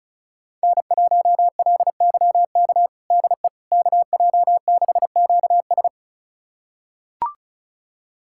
Morse code, heard as a keyed tone, about 36 wpm, 700 Hz. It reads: N1LYK DE KJ6QS
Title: USA Callsign DE Callsign - 5 Characters - 35wpm